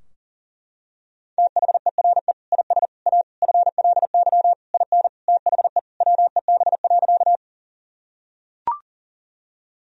Morse code, heard as a keyed tone, about 40 wpm, 700 Hz. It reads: THERE IS A FLY IN THE WEB.
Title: Sentences from Top 1000 Words - Review Part 4 - 40wpm